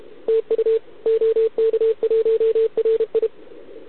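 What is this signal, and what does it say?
Morse code, heard as a keyed tone, about 30 wpm, 440 Hz, with radio noise. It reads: TU OK1RI
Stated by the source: Contest CW from OK1RI on 80 m during the OKOM DX contest, recorded by a QRP station using a KX3 and a wire antenna.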